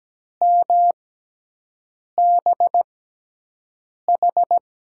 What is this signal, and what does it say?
Morse code, heard as a keyed tone, about 17 wpm, spaced as for 6 wpm, 700 Hz. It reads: MBH
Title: USA Call Signs - 5 Characters - 6x Character Spacing - 17wpm